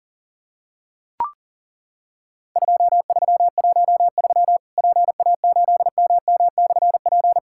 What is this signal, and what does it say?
Morse code, heard as a keyed tone, about 41 wpm, 700 Hz. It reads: 2313 PA8MM/P
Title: Worked All Europe DX Contest - QTC Exchange - 40wpm